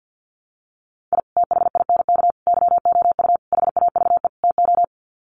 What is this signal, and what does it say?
Morse code, heard as a keyed tone, about 50 wpm, 700 Hz. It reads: I THINK YOU HAVE TO
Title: Sets of 5 Words 50wpm